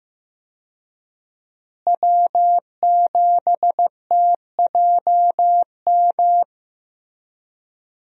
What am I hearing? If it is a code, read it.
Morse code, heard as a keyed tone, about 15 wpm, 700 Hz. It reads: W7TJM
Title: USA Call Signs - 5 Characters - Repeated 3x - 15wpm